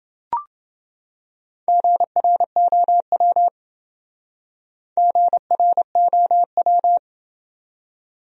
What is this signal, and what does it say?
Morse code, heard as a keyed tone, about 28 wpm, 700 Hz. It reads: GROW GROW